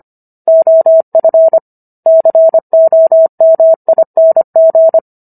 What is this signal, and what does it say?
Morse code, heard as a keyed tone, about 25 wpm, 650 Hz. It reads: OF COMING